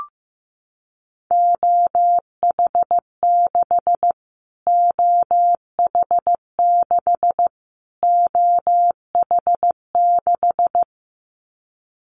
Morse code, heard as a keyed tone, about 15 wpm, 700 Hz. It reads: OH6 OH6 OH6
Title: Callsign Prefixes for Top 25 DXCC - Repeated 3x - 15wpm